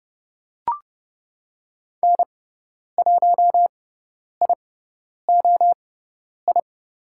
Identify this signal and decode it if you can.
Morse code, heard as a keyed tone, about 30 wpm, spaced as for 10 wpm, 700 Hz. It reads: N1IOI
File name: Lesson 16g - Letter c - Call Signs 30wpm